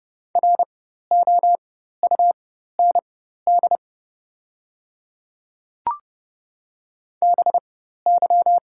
Morse code, heard as a keyed tone, about 31 wpm, spaced as for 14 wpm, 700 Hz. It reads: ROUND BY
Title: 4x Character Spacing - Top 300 Words - 30wpm